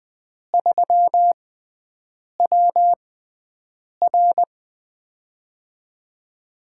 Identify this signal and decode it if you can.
Morse code, heard as a keyed tone, about 20 wpm, spaced as for 7 wpm, 700 Hz. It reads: 3WR
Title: USA Call Signs - 5 Characters - 6x Character Spacing - 20wpm